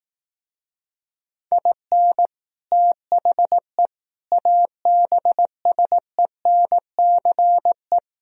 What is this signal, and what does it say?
Morse code, heard as a keyed tone, about 18 wpm, 700 Hz. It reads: IN THE ABSENCE
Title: Speed-Racing - Sets of 4 Words 15wpm